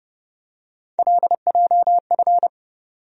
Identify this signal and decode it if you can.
Morse code, heard as a keyed tone, about 30 wpm, 700 Hz. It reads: LJF
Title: USA Call Sign Postfixes - 3 Letters 30wpm